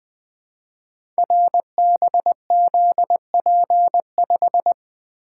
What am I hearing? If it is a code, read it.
Morse code, heard as a keyed tone, about 20 wpm, 700 Hz. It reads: RBZP5
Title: Code Groups - 5 Letter:Numbers - 20wpm